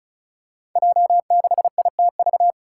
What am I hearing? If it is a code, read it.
Morse code, heard as a keyed tone, about 36 wpm, 700 Hz. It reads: J6ITV
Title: Five Challenging Letter-Number Combinations 35wpm